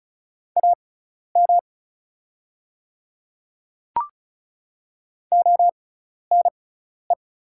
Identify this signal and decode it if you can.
Morse code, heard as a keyed tone, about 36 wpm, spaced as for 12 wpm, 700 Hz. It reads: AM ONE
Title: 6x Character Spacing - Top 500 Words - 35wpm